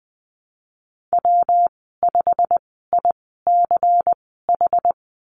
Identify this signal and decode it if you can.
Morse code, heard as a keyed tone, about 20 wpm, spaced as for 15 wpm, 700 Hz. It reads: W5ICH
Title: 161 Lesson 26g - Letter g - Call Signs 20wpm